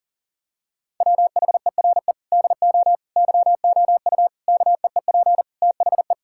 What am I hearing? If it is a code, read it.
Morse code, heard as a keyed tone, about 37 wpm, 700 Hz. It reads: WHERE DO YOU KEEP THE
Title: Sentences from Top 1000 Words - Review Part 5 - 40wpm